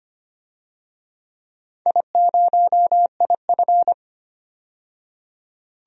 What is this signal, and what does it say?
Morse code, heard as a keyed tone, about 25 wpm, 700 Hz. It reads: I0IF
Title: DXCC-100 Call Signs - 4 Characters - 25wpm